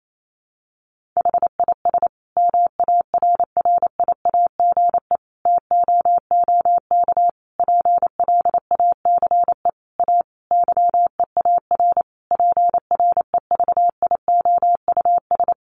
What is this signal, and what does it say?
Morse code, heard as a keyed tone, about 28 wpm, 700 Hz. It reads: HIS MARRIAGE TOOK PLACE A YEAR PREVIOUS
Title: Sentences from Top 1500 Words 28wpm